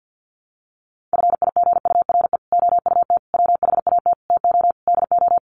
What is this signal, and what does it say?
Morse code, heard as a keyed tone, about 50 wpm, 700 Hz. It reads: FIGURE OUT WHAT TO DO
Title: Sets of 5 Words Encore2 50wpm